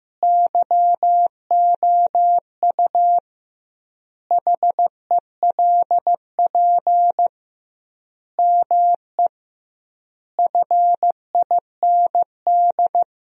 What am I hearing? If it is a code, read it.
Morse code, heard as a keyed tone, about 15 wpm, 700 Hz. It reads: YOU HELP ME FIND
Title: 2x Word-Spacing - Sentences from Top 200 Words - 15wpm